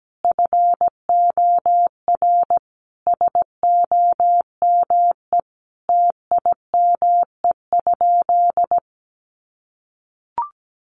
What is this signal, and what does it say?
Morse code, heard as a keyed tone, about 17 wpm, 700 Hz. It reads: FOR SOME TIME?
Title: Sentences from Top 600 Words 17wpm